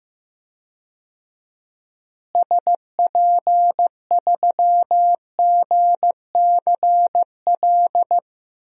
Morse code, heard as a keyed tone, about 15 wpm, 700 Hz. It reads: SP3GCL
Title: DXCC-100 Call Signs - 6 Characters - Repeated 2x - 15wpm